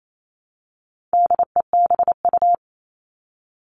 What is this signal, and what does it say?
Morse code, heard as a keyed tone, about 28 wpm, 700 Hz. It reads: DEBU
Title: All Cut Numbers - 4-Digit 28wpm